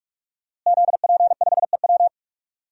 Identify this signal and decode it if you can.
Morse code, heard as a keyed tone, about 46 wpm, 700 Hz. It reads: ZP5EW
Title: DXCC-100 Call Signs - 5 Characters - Repeated 2x - 45wpm